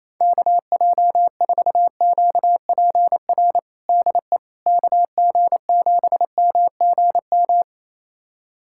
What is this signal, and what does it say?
Morse code, heard as a keyed tone, about 28 wpm, 700 Hz. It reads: KJ4QPR DE KG7MGM
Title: USA Callsign DE Callsign 28wpm